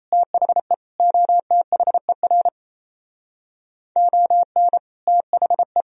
Morse code, heard as a keyed tone, about 37 wpm, 700 Hz. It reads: THE OTHER ON THE